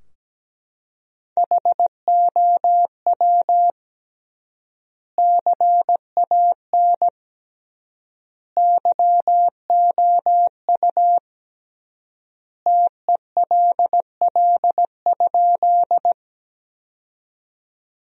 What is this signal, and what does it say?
Morse code, heard as a keyed tone, about 17 wpm, 700 Hz. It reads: HOW CAN YOU TELL?
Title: 3x Word-Spacing - Sentences from Top 200 Words - 17wpm